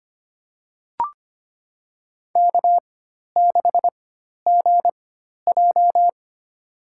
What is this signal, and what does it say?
Morse code, heard as a keyed tone, about 25 wpm, spaced as for 12 wpm, 690 Hz. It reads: K6GJ
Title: USA Call Signs - 4 Characters - 4x Character Spacing - 25wpm